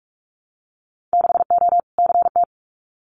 Morse code, heard as a keyed tone, about 44 wpm, 700 Hz. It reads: 6O CT
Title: Winter Field Day Exchanges Encore 45wpm